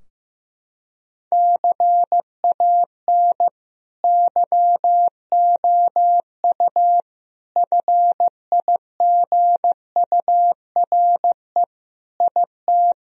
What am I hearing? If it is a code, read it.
Morse code, heard as a keyed tone, about 15 wpm, 700 Hz. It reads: CAN YOU FIGURE IT
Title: Sentences from Top 1000 Words - Review Part 2 - 15wpm